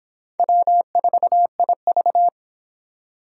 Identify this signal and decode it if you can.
Morse code, heard as a keyed tone, about 23 wpm, 700 Hz. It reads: W4IV